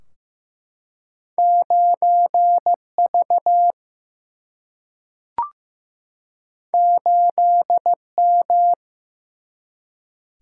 Morse code, heard as a keyed tone, about 15 wpm, 700 Hz. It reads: 9V 8M
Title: Single-Letter Single-Number 15wpm